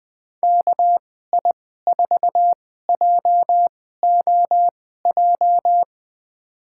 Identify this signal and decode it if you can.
Morse code, heard as a keyed tone, about 21 wpm, spaced as for 15 wpm, 700 Hz. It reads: KI4JOJ
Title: USA Call Signs - 2x Character Spacing - 20wpm